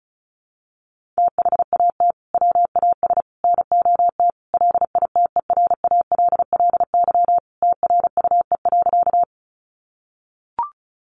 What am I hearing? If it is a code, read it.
Morse code, heard as a keyed tone, about 34 wpm, 700 Hz. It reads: THAT WAS NOT LITERALLY TRUE.
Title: Sentences from Top 1400 Words 35wpm